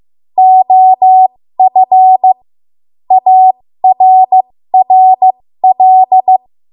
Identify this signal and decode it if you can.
Morse code, heard as a keyed tone, about 15 wpm, spaced as for 13 wpm, 750 Hz. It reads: OF ARRL
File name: Morse / 13WPM